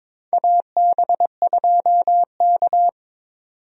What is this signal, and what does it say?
Morse code, heard as a keyed tone, about 22 wpm, 700 Hz. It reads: AB2K
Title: USA Call Signs - 4 Characters - 22wpm